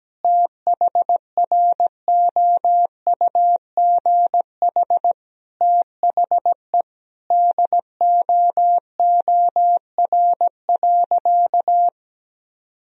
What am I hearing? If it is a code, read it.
Morse code, heard as a keyed tone, about 17 wpm, 700 Hz. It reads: THROUGH THE DOOR.
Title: Sentences from Top 400 Words and Abbreviations 17wpm